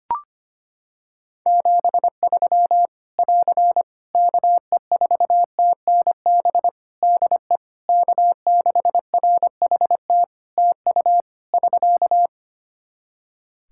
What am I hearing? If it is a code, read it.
Morse code, heard as a keyed tone, about 25 wpm, 700 Hz. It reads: 73 <AR> KE4TNB DE K6RHT TU <SK>
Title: Practice QSO - Exchange 4 - Ending - 25wpm